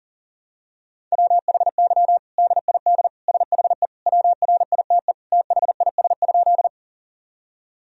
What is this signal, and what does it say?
Morse code, heard as a keyed tone, about 40 wpm, 700 Hz. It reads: WHY DID SHE WRITE THIS?
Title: Sentences from Top 1000 Words - Review Part 1 - 40wpm